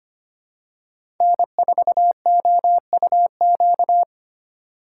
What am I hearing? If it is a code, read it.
Morse code, heard as a keyed tone, about 25 wpm, 700 Hz. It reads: N4OUQ
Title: USA Call Signs - 5 Characters - 8x Character Spacing - 25wpm